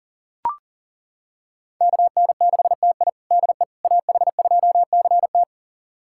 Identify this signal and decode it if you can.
Morse code, heard as a keyed tone, about 40 wpm, 700 Hz. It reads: KN6TI DE AH2CT
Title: USA Callsign DE Callsign - 5 Characters - 40wpm